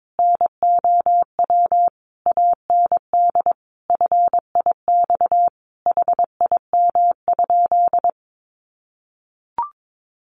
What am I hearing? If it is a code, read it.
Morse code, heard as a keyed tone, about 22 wpm, 700 Hz. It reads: NOW AND FIX HIM?
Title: Sentences from Top 1400 Words 22wpm